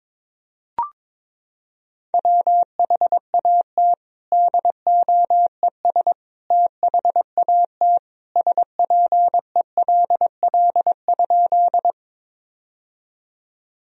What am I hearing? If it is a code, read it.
Morse code, heard as a keyed tone, about 22 wpm, 700 Hz. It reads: WHAT DOES THAT SPELL?
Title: Sentences from Top 1000 Words - Review Part 1 - 22wpm